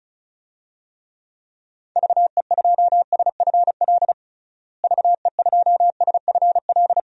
Morse code, heard as a keyed tone, about 34 wpm, 700 Hz. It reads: VE2SFL VE2SFL
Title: DXCC-100 Call Signs - 6 Characters - Repeated 3x - 35wpm